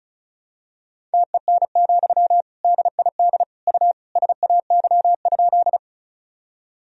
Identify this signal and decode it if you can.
Morse code, heard as a keyed tone, about 34 wpm, 700 Hz. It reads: TEN, DID U SAY?
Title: Sentences from Top 400 Words and Abbreviations 35wpm